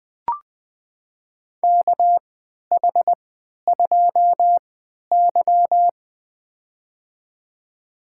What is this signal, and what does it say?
Morse code, heard as a keyed tone, about 21 wpm, spaced as for 11 wpm, 700 Hz. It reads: KH2Y
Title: USA Call Signs - 4 Characters - 3x Character Spacing - 20wpm